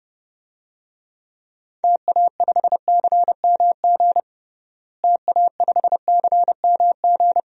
Morse code, heard as a keyed tone, about 30 wpm, 700 Hz. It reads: TA5CMG TA5CMG
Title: DXCC-100 Call Signs - 6 Characters - Repeated 2x - 30wpm